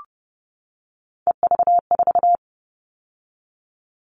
Morse code, heard as a keyed tone, about 30 wpm, 700 Hz. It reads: EV4
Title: All Cut Numbers - 3-Digit 30wpm